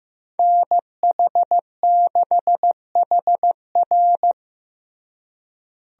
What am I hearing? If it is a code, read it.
Morse code, heard as a keyed tone, about 15 wpm, 700 Hz. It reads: NH6HR
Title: USA Call Signs - 5 Characters - Repeated 3x - 15wpm